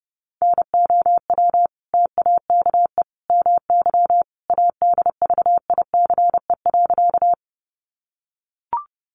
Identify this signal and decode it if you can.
Morse code, heard as a keyed tone, about 30 wpm, 700 Hz. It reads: NOW TAKE MY ADVICE.
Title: Sentences from Top 1100 Words 30wpm